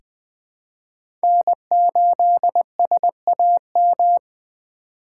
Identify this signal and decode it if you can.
Morse code, heard as a keyed tone, about 20 wpm, 700 Hz. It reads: N8SAM
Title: USA Call Signs - 5 Characters - 3x Character Spacing - 20wpm